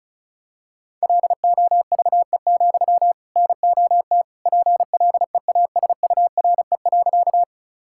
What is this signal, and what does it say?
Morse code, heard as a keyed tone, about 36 wpm, 700 Hz. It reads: LOVE, NOT PLEASURE.
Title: Sentences from Top 1600 Words 35wpm